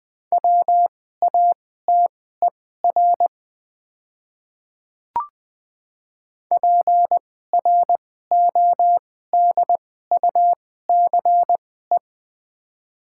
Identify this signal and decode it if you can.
Morse code, heard as a keyed tone, about 20 wpm, spaced as for 15 wpm, 700 Hz. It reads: WATER PRODUCE
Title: 2x Character Spacing - Top 500 Words - 20wpm